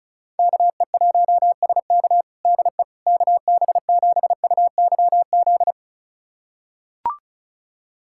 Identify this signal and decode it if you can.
Morse code, heard as a keyed tone, about 34 wpm, 700 Hz. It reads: KE1SK DE KB7UYZ
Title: USA Callsign DE Callsign 35wpm